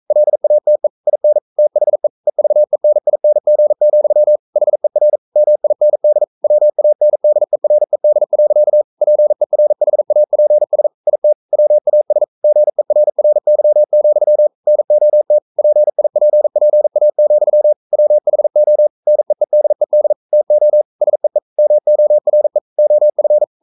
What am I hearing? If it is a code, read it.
Morse code, heard as a keyed tone, about 42 wpm, 600 Hz. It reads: LATE IN THE EVENING, HER MIND WANDERED. PERHAPS IT WAS GERRY, NOT PIPPA, WHO NEEDED TO SEE MORE OF
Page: Romanen Never af Ken Follet på engelsk.